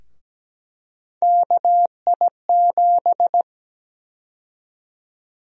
Morse code, heard as a keyed tone, about 17 wpm, 700 Hz. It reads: KI7
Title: USA Call Sign Prefixes - Repeated 2x - 17wpm